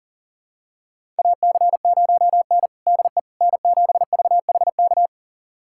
Morse code, heard as a keyed tone, about 39 wpm, 700 Hz. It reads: AC0N DE N7VHK
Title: USA Callsign DE Callsign 40wpm